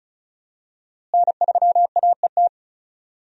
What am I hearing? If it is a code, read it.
Morse code, heard as a keyed tone, about 34 wpm, 700 Hz. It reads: N3AET
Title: USA Call Signs - 5 Characters - Repeated 2x - 35wpm